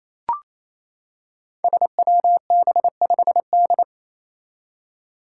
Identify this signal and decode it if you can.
Morse code, heard as a keyed tone, about 27 wpm, 700 Hz. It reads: SWB5D
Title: Five Challenging Letter-Number Combinations 28wpm